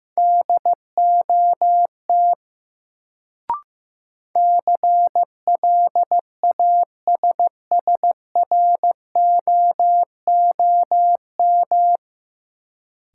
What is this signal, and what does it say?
Morse code, heard as a keyed tone, about 15 wpm, 700 Hz. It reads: DOT CLASSROOM
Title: Sentences from Top 1500 Words - New Word Drill - 15wpm